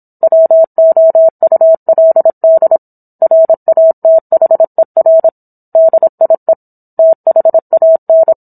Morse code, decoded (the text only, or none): WOULD RATHER DIE THAN